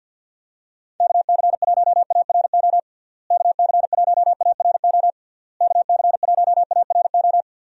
Morse code, heard as a keyed tone, about 50 wpm, 700 Hz. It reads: KC1ARO KC1ARO KC1ARO
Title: USA Call Signs - 6 Characters - Repeated 3x - 50wpm